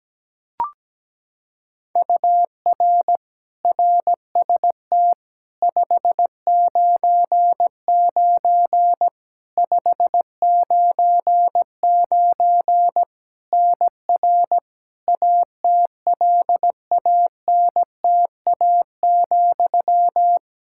Morse code, heard as a keyed tone, about 17 wpm, 700 Hz. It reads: UR RST 599 599 NR ATLANTA,
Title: Practice QSO - Exchange 1 - Beginning - 17wpm